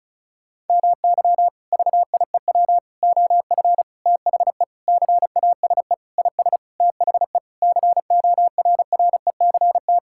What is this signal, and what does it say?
Morse code, heard as a keyed tone, about 36 wpm, 700 Hz. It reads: MY VIEW OF THE CASE IS THE CORRECT
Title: Sentences from Top 1000 Words - Review Part 6 - 35wpm